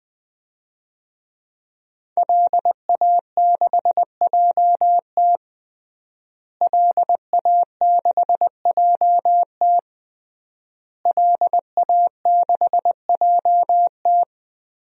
Morse code, heard as a keyed tone, about 20 wpm, 700 Hz. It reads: LA6JT LA6JT LA6JT
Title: DXCC-100 Call Signs - 5 Characters - Repeated 3x - 20wpm